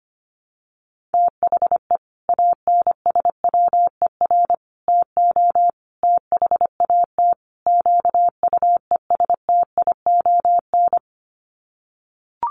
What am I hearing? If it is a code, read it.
Morse code, heard as a keyed tone, about 25 wpm, 700 Hz. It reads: THE ANSWER TO THAT QUESTION
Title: Sets of 5 Words Encore2 25wpm